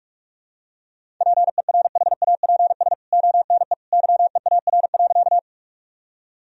Morse code, heard as a keyed tone, about 46 wpm, 700 Hz. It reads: PERHAPS ONE YEAR.
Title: Sentences from Top 1000 Words - Review Part 1 - 45wpm